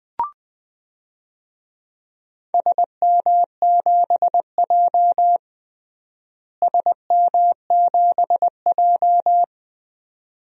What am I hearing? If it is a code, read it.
Morse code, heard as a keyed tone, about 20 wpm, 700 Hz. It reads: SM7J SM7J
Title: DXCC-100 Call Signs - 4 Characters - Repeated 2x - 20wpm